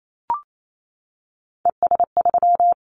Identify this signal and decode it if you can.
Morse code, heard as a keyed tone, about 28 wpm, 700 Hz. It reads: ES3
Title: Callsign Prefixes for Top 100 DXCC - 28wpm